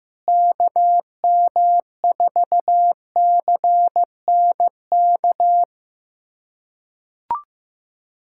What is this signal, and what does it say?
Morse code, heard as a keyed tone, about 15 wpm, 700 Hz. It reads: KM4CNK
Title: USA Callsign DE Callsign - 6 Characters - 15wpm